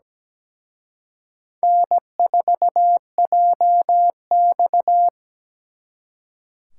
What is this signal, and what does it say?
Morse code, heard as a keyed tone, about 20 wpm, 700 Hz. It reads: N4JX
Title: Speed Racing - USA 1x2 Call Signs 17wpm